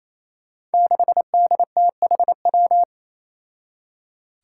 Morse code, heard as a keyed tone, about 27 wpm, 700 Hz. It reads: 6DTHW
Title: Five Challenging Letter-Number Combinations 28wpm